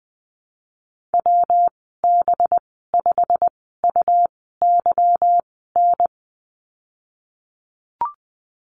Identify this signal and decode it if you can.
Morse code, heard as a keyed tone, about 19 wpm, spaced as for 14 wpm, 700 Hz. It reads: WB5UYN
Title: 219 Lesson 35g - Number 0 - Call Signs 20wpm